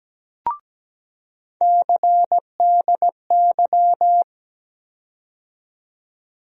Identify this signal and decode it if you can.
Morse code, heard as a keyed tone, about 17 wpm, 700 Hz. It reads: CDY
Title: USA Call Sign Postfixes - 3 Letters 17wpm